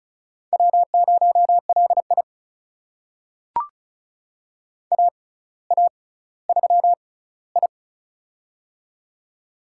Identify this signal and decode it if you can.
Morse code, heard as a keyed tone, about 36 wpm, spaced as for 18 wpm, 700 Hz. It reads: W0LI AA3I
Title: USA Call Signs - 4 Characters - 6x Character Spacing - 35wpm